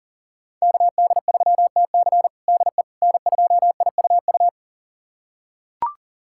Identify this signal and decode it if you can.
Morse code, heard as a keyed tone, about 40 wpm, 700 Hz. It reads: KD3TC DE N2IUU
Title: USA Callsign DE Callsign - 5 Characters - 40wpm